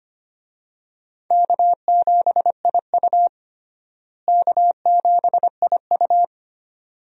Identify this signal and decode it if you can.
Morse code, heard as a keyed tone, about 25 wpm, 700 Hz. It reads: K7IU K7IU
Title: USA Call Signs - 4 Characters - Repeated 2x - 25wpm